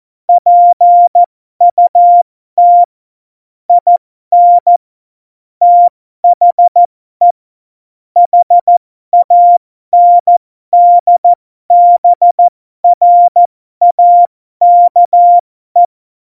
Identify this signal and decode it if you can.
Morse code, heard as a keyed tone, about 14 wpm, spaced as for 12 wpm, 700 Hz. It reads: PUT IN THE HANDBRAKE